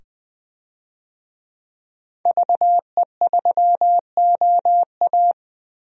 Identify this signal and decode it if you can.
Morse code, heard as a keyed tone, about 21 wpm, 700 Hz. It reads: VE3OA
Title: DXCC-100 Call Signs - 5 Characters - 20wpm